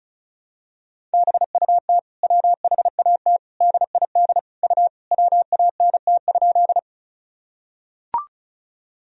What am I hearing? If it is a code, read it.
Morse code, heard as a keyed tone, about 36 wpm, 700 Hz. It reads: BUT WHAT DID U WANT?
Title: Sentences from Top 200 Words and Abbreviations 35wpm